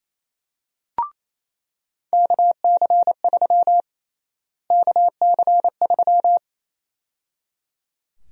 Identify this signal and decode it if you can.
Morse code, heard as a keyed tone, about 27 wpm, 700 Hz. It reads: KC3 KC3
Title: USA Call Sign Prefixes - Repeated 2x - 28wpm